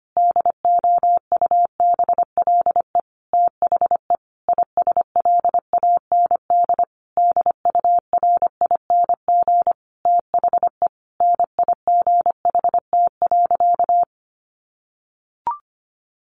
Morse code, heard as a keyed tone, about 25 wpm, 700 Hz. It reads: DOUBLE THE ISLAND DURING THE NIGHT.
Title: Sentences from Top 1400 Words 25wpm